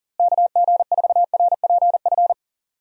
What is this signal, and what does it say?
Morse code, heard as a keyed tone, about 40 wpm, 700 Hz. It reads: KG4RPF
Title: USA Call Signs - 6 Characters - Repeated 3x - 40wpm